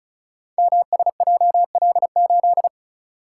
Morse code, heard as a keyed tone, about 36 wpm, 700 Hz. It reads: MSJL8
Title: Code Groups - 5 Letter:Numbers - 35wpm